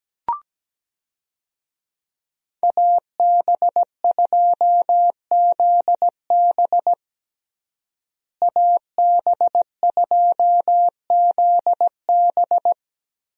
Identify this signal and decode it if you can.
Morse code, heard as a keyed tone, about 17 wpm, 700 Hz. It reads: AB2ZB AB2ZB
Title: USA Call Signs - 5 Characters - Repeated 3x - 17wpm